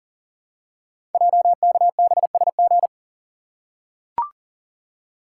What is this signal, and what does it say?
Morse code, heard as a keyed tone, about 38 wpm, 700 Hz. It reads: JKBSG
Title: Code Groups - 5 Letters - 40wpm